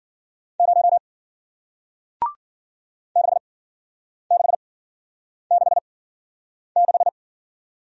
Morse code, heard as a keyed tone, about 53 wpm, spaced as for 9 wpm, 700 Hz. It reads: 0 6666